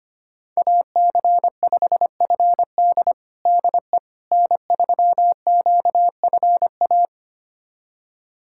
Morse code, heard as a keyed tone, about 25 wpm, 700 Hz. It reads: AC5FD DE N3QFA
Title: USA Callsign DE Callsign - 5 Characters - 25wpm